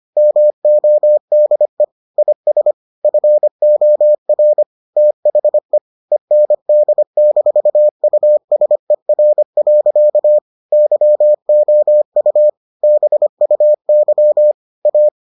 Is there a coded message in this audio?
Yes: Morse code from 0.2 to 15.1 s, heard as a keyed tone, about 25 wpm, 600 Hz. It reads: MODE IS FOR THE END-USER. YOU BUY A